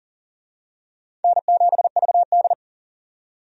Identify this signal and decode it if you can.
Morse code, heard as a keyed tone, about 40 wpm, 700 Hz. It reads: N7VD
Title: USA Call Signs - 4 Characters - 40wpm